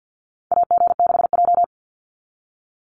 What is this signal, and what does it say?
Morse code, heard as a keyed tone, about 49 wpm, 700 Hz. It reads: AG6J
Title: USA Call Signs 50wpm